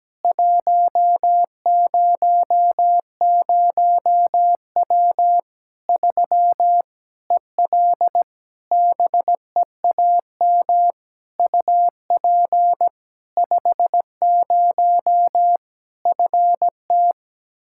Morse code, heard as a keyed tone, about 17 wpm, 700 Hz. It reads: 100W 3 EL BEAM UP 50 FT
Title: K1USN SST QSO Practice 17wpm